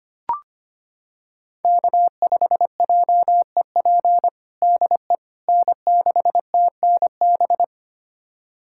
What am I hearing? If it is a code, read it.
Morse code, heard as a keyed tone, about 25 wpm, 700 Hz. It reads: K5JEP DE N6TNB
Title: USA Callsign DE Callsign - 5 Characters - 25wpm